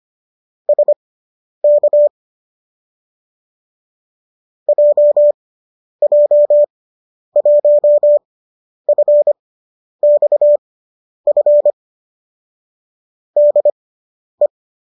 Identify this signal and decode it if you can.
Morse code, heard as a keyed tone, about 25 wpm, spaced as for 10 wpm, 590 Hz. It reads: SK JJ1FXF DE
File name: qso005 10